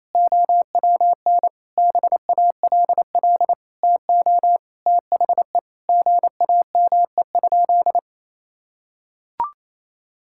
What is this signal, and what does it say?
Morse code, heard as a keyed tone, about 28 wpm, 700 Hz. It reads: OWN BALL TO THE GAME?
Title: Sentences from Top 1000 Words - Review Part 6 - 28wpm